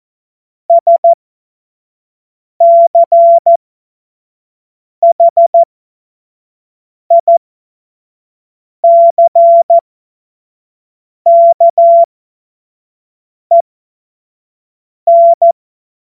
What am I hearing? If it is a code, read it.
Morse code, heard as a keyed tone, about 14 wpm, spaced as for 5 wpm, 680 Hz. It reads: SCHICKEN
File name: Morsekurs / Witze / Witz 05 Dezember 2023 25 F70